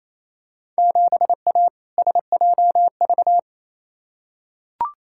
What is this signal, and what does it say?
Morse code, heard as a keyed tone, about 28 wpm, 700 Hz. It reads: 7A SJV
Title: Field Day Exchanges Encore - 28wpm